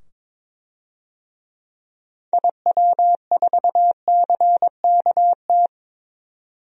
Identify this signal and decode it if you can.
Morse code, heard as a keyed tone, about 22 wpm, 700 Hz. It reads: IW4CKT
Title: DXCC-100 Call Signs - 6 Characters - 22wpm